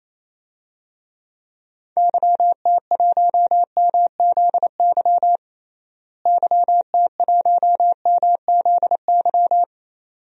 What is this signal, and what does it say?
Morse code, heard as a keyed tone, about 28 wpm, 700 Hz. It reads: YT1MZY YT1MZY
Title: DXCC-100 Call Signs - 6 Characters - Repeated 2x - 28wpm